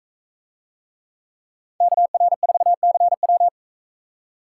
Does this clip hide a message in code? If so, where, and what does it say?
Yes: Morse code from 1.8 to 3.5 s, heard as a keyed tone, about 44 wpm, 700 Hz. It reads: KR4CW